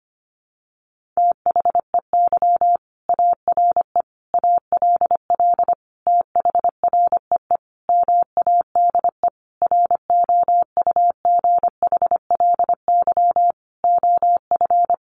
Morse code, heard as a keyed tone, about 25 wpm, 700 Hz. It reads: THEY ARE ALL THREE MADE ROUGHLY OF
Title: Sentences from Top 1500 Words 25wpm